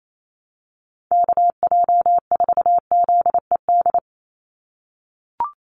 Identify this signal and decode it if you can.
Morse code, heard as a keyed tone, about 28 wpm, 700 Hz. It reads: KJ4ZED
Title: USA Call Signs 28wpm